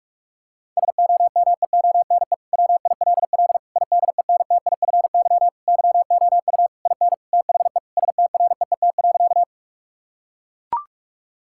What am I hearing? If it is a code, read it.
Morse code, heard as a keyed tone, about 41 wpm, 700 Hz. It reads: SOMEONE WILL IDENTIFY YOU IN THE STREET.
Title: Sentences from Top 1000 Words - Review Part 6 - 45wpm